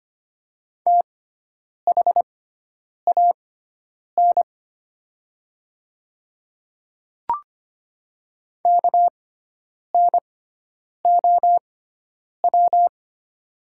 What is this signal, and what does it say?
Morse code, heard as a keyed tone, about 25 wpm, spaced as for 9 wpm, 700 Hz. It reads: THAN KNOW
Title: 6x Character Spacing - Top 500 Words - 25wpm